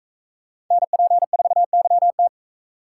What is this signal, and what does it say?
Morse code, heard as a keyed tone, about 41 wpm, 700 Hz. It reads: NP4YT